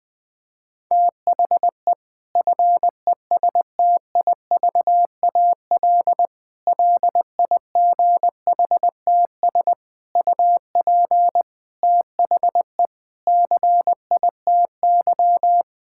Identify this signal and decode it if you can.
Morse code, heard as a keyed tone, about 20 wpm, 700 Hz. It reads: THE FESTIVAL LIGHTS UP THE CITY
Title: Difficult Sentences with Hints - 20wpm